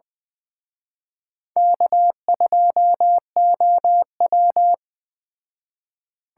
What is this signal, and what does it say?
Morse code, heard as a keyed tone, about 20 wpm, 700 Hz. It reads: K2OW
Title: Speed Racing - USA 1x2 Call Signs 17wpm